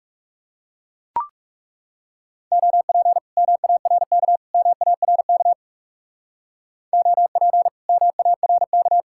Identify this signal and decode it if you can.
Morse code, heard as a keyed tone, about 39 wpm, 700 Hz. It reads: OP MARK MARK OP MARK